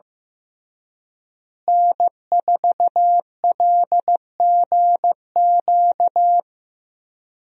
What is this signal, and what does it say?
Morse code, heard as a keyed tone, about 15 wpm, 700 Hz. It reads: N4LGQ
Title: USA Call Signs - 5 Characters - Repeated 2x - 15wpm